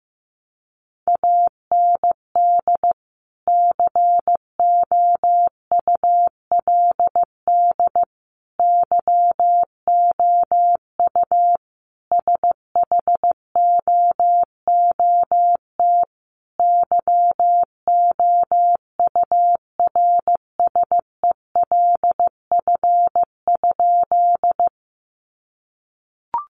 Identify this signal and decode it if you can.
Morse code, heard as a keyed tone, about 15 wpm, 700 Hz. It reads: AND COULD YOU SHOOT YOURSELF?
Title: Sentences from Top 1200 Words 15wpm